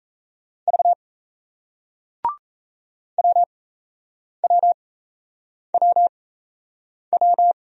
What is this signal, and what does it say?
Morse code, heard as a keyed tone, about 38 wpm, spaced as for 8 wpm, 700 Hz. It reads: V WWWW